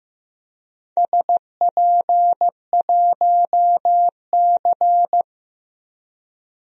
Morse code, heard as a keyed tone, about 15 wpm, 700 Hz. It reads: SP1C
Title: DXCC-100 Call Signs - 4 Characters - 15wpm